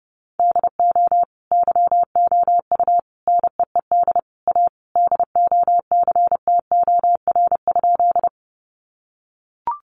Morse code, heard as a keyed tone, about 30 wpm, 700 Hz. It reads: DO YOU NEED A DOCTOR?
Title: Sentences from Top 1500 Words 30wpm